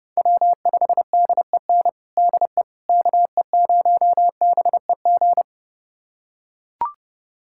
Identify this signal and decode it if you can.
Morse code, heard as a keyed tone, about 31 wpm, 700 Hz. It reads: W5DEN DE KE0BEG
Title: USA Callsign DE Callsign 30wpm